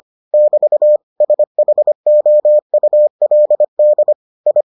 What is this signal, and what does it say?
Morse code, heard as a keyed tone, about 25 wpm, 600 Hz. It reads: <BT> SHOULD I